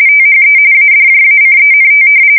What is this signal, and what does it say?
RTTY, about 45.45 baud. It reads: THE ADJACENT
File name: fsk170 / usnavy3